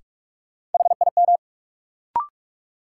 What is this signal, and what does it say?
Morse code, heard as a keyed tone, about 43 wpm, 700 Hz. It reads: HIM
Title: Sentences from Top 100 Words - New Word Drill - 45wpm